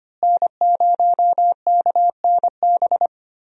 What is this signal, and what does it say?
Morse code, heard as a keyed tone, about 25 wpm, 700 Hz. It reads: N0KNB
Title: USA Call Signs - 5 Characters - Repeated 2x - 25wpm